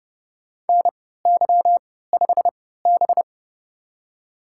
Morse code, heard as a keyed tone, about 30 wpm, spaced as for 17 wpm, 690 Hz. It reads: NY5B
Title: USA Call Signs - 3x Character Spacing - 30wpm